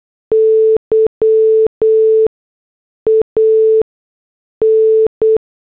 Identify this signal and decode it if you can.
Morse code, heard as a keyed tone, about 8 wpm, spaced as for 6 wpm, 440 Hz. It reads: YAN